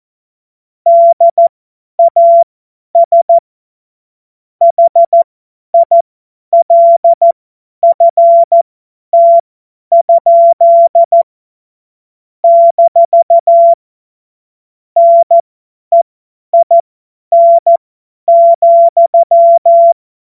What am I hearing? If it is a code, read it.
Morse code, heard as a keyed tone, about 14 wpm, spaced as for 10 wpm, 680 Hz. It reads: DAS HILFT? - NEIN,
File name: Morsekurs / Witze / Witz 26 März 2024 50 F70